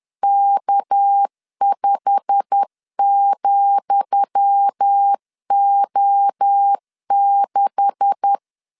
Morse code, heard as a keyed tone, about 10 wpm, 800 Hz. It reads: K5,O6
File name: LEZIONI / 10 SIGNS / 10D COMA 10wpm 3013